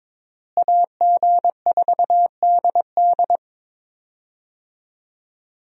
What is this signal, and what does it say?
Morse code, heard as a keyed tone, about 22 wpm, 700 Hz. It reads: AG4DD
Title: USA Call Signs - 5 Characters - Repeated 3x - 22wpm